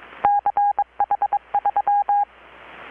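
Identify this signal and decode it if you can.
Morse code, heard as a keyed tone, about 21 wpm, 820 Hz. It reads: CH3
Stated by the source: Morse from KPH's Night of Nights XX maritime radiotelegraphy event, received in Chiapas, Mexico about 3,800 km away on a Kenwood R-600 with a grounded 10 m long-wire antenna.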